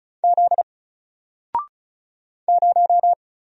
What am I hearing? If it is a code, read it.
Morse code, heard as a keyed tone, about 36 wpm, 700 Hz. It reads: Z 0
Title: Single Letter-Number 35wpm